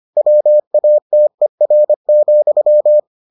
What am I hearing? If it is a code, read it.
Morse code, heard as a keyed tone, about 25 wpm, 600 Hz. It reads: WATER,